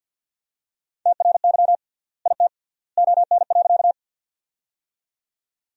Morse code, heard as a keyed tone, about 51 wpm, 700 Hz. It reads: TRY IT ON.
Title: 3x Word-Spacing - Sentences from Top 200 Words - 50wpm